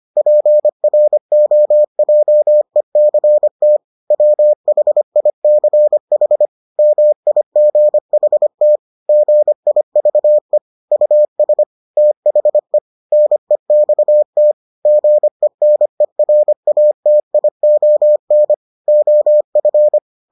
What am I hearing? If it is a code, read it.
Morse code, heard as a keyed tone, about 25 wpm, 600 Hz. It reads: PROJECT WHICH MIGHT GIVE US THE NEXT GENERATION OF